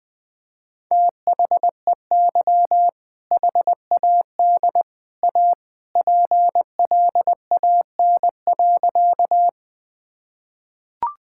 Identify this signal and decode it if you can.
Morse code, heard as a keyed tone, about 20 wpm, 700 Hz. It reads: THEY HAD A PLAN.
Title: Sentences with 2-Word Phrases - Spoken Beginning - 20wpm